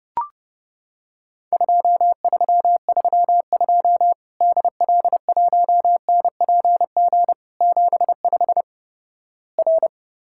Morse code, heard as a keyed tone, about 30 wpm, 700 Hz. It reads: 2332 DL1NPG 75 R
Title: Worked All Europe DX Contest - QTC Exchange - 30wpm